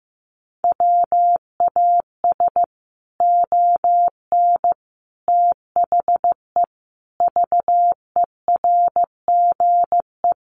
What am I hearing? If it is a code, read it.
Morse code, heard as a keyed tone, about 15 wpm, 700 Hz. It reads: WAS ON THE VERGE
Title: Sets of 5 Words Encore 15wpm